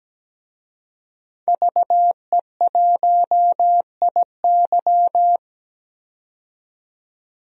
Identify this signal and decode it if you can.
Morse code, heard as a keyed tone, about 17 wpm, 700 Hz. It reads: VE1IY
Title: DXCC-100 Call Signs - 5 Characters - 17wpm